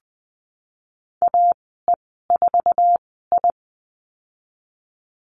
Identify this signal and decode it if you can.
Morse code, heard as a keyed tone, about 20 wpm, spaced as for 15 wpm, 700 Hz. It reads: AE4I
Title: 186 Lesson 30g - Letter b - Call Signs 20wpm